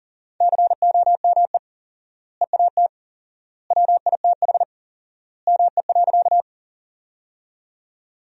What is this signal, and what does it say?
Morse code, heard as a keyed tone, about 40 wpm, 700 Hz. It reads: COME EAT WITH ME.
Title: 4x Word-Spacing - Sentences from Top 300 Words - 40wpm